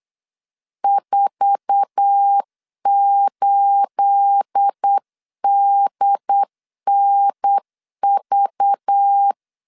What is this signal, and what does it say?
Morse code, heard as a keyed tone, about 8 wpm, 800 Hz. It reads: 48DNV
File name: LEZIONI / 9 28 8wpm 3512